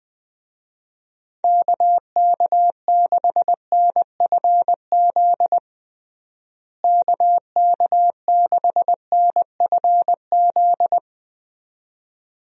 Morse code, heard as a keyed tone, about 20 wpm, 700 Hz. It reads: KK6NFZ KK6NFZ
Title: USA Call Signs - 6 Characters - Repeated 2x - 20wpm